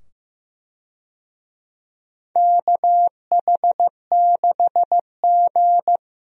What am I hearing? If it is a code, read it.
Morse code, heard as a keyed tone, about 15 wpm, 700 Hz. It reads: KH6G
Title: DXCC-100 Call Signs - 4 Characters - 15wpm